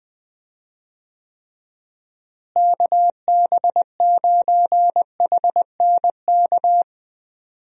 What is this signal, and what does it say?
Morse code, heard as a keyed tone, about 20 wpm, 700 Hz. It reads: KB9HNK
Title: USA Call Signs - 6 Characters - Repeated 3x - 20wpm